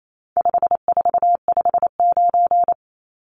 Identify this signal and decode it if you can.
Morse code, heard as a keyed tone, about 28 wpm, 700 Hz. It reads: 5459
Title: Numbers - 4-Digit 28wpm